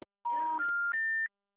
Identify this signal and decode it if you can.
Special information tone.
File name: out- -2010- - - .86756